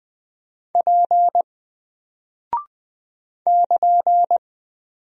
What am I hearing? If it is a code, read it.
Morse code, heard as a keyed tone, about 20 wpm, 700 Hz. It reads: P <KN>
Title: Single Character (Letter:Number:Punctuation:Prosign) 20wpm